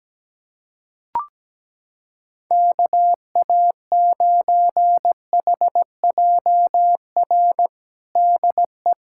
Morse code, heard as a keyed tone, about 17 wpm, 700 Hz. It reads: KA9HJR DE
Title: Practice QSO - Exchange 4 - Ending - 17wpm